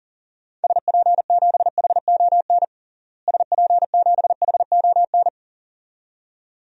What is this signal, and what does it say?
Morse code, heard as a keyed tone, about 40 wpm, 700 Hz. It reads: SP7HON SP7HON
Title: DXCC-100 Call Signs - 6 Characters - Repeated 2x - 40wpm